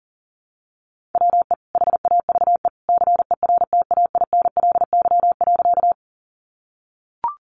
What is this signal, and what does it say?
Morse code, heard as a keyed tone, about 40 wpm, 700 Hz. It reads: WE HAVE CERTAINLY.
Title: Sentences from Top 900 Words 40wpm